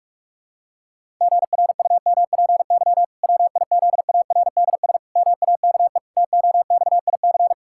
Morse code, heard as a keyed tone, about 46 wpm, 700 Hz. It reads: GRUMPY WIZARDS MAKE TOXIC
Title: Pangram Challenge v2 - 45wpm